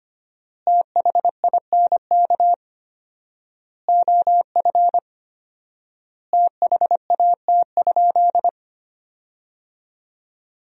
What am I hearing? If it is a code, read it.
Morse code, heard as a keyed tone, about 25 wpm, 700 Hz. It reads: THINK OF THAT?
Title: 4x Word-Spacing - Sentences from Top 200 Words - 25wpm